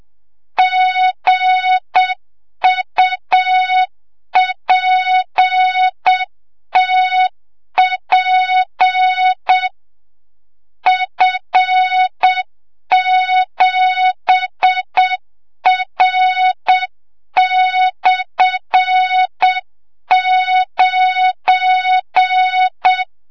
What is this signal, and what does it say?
Morse code, heard as a keyed tone, about 7 wpm, 3700 Hz, with noise. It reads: GUPTP F7R/9
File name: Morze-HA2MN / hang / 25 GYAKORLAT